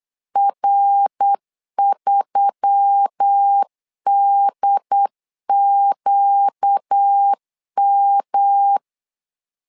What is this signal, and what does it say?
Morse code, heard as a keyed tone, about 8 wpm, 800 Hz. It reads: R3DQM
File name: LEZIONI / 4 Q1S3 8wpm 3037